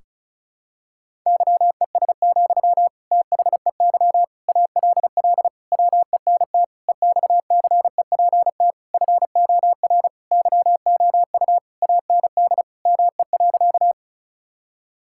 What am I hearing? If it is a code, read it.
Morse code, heard as a keyed tone, about 34 wpm, 700 Hz. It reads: YES, THEY ALL WENT EXCEPT FOR YOU AND ME.
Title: Sentences from Top 1600 Words 35wpm